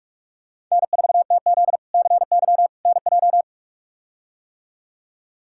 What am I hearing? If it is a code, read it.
Morse code, heard as a keyed tone, about 47 wpm, 700 Hz. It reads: N4TZ CY NJ
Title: K1USN SST QSO Practice - 45wpm